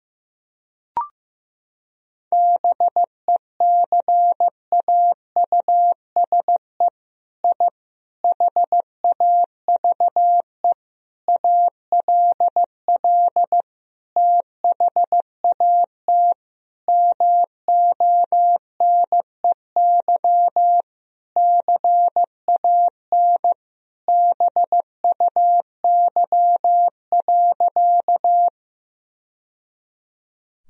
Morse code, heard as a keyed tone, about 15 wpm, 700 Hz. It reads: BECAUSE I HAVE ALL THAT MONEY CAN BUY.
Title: Sentences from Top 1000 Words - Review Part 4 - 15wpm